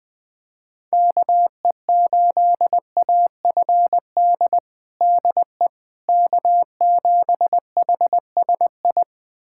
Morse code, heard as a keyed tone, about 20 wpm, 700 Hz. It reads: KE8AFD DE K7HSI
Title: USA Callsign DE Callsign 20wpm